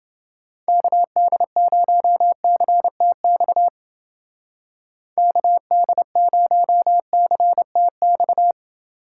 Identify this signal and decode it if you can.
Morse code, heard as a keyed tone, about 29 wpm, 700 Hz. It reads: KD0CTX KD0CTX